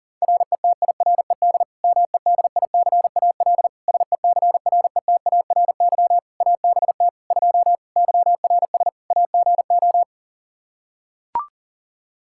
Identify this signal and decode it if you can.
Morse code, heard as a keyed tone, about 40 wpm, 700 Hz. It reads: RETIRED MEDICAL SECRETARY ABT 2 YRS AGO
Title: Practice QSO - Exchange 3 - More Info - 40wpm